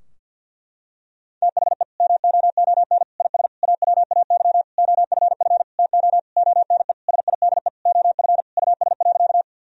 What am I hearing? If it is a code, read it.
Morse code, heard as a keyed tone, about 52 wpm, 700 Hz. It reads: THE MOON IS AWAY OFF TO ONE SIDE OF US.
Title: Sentences from Top 1000 Words - Review Part 3 - 50wpm